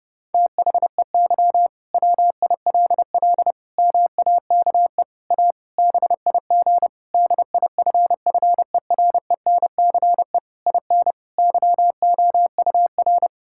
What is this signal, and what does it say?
Morse code, heard as a keyed tone, about 30 wpm, 700 Hz. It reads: THEY WILL MAKE A BIG DIFFERENCE IN YOUR
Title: Sentences from Top 1000 Words - Review Part 5 - 30wpm